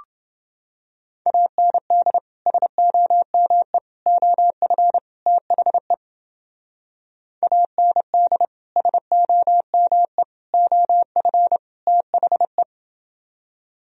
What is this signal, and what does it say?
Morse code, heard as a keyed tone, about 29 wpm, 700 Hz. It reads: AND SOME OF THE AND SOME OF THE